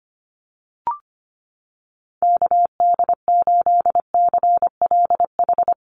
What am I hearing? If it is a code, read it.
Morse code, heard as a keyed tone, about 25 wpm, 700 Hz. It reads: KD8CLH
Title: USA Call Signs Repeated 2x 25wpm